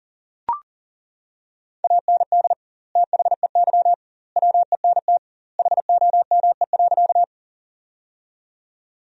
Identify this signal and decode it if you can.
Morse code, heard as a keyed tone, about 38 wpm, 700 Hz. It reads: AND THEY WENT HOME.
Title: 2x Word-Spacing - Sentences from Top 200 Words - 40wpm